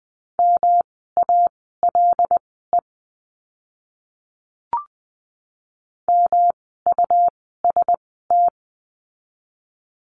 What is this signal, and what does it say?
Morse code, heard as a keyed tone, about 20 wpm, spaced as for 15 wpm, 700 Hz. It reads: MALE MUST
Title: 108 Lesson 18f - Letter m - Words 20wpm